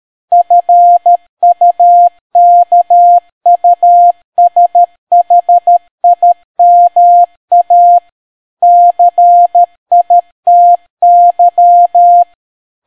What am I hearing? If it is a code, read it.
Morse code, heard as a keyed tone, about 13 wpm, 700 Hz. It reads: FUKUSHIMA CITY